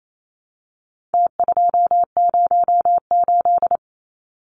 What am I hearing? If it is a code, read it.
Morse code, heard as a keyed tone, about 28 wpm, 700 Hz. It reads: T208
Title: Numbers - Contest Style - 4-Digit 28wpm